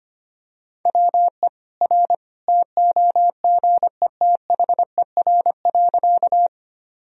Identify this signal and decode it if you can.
Morse code, heard as a keyed tone, about 25 wpm, 700 Hz. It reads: WE R TOGETHER.
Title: Sentences from Top 300 Words and Abbreviations 25wpm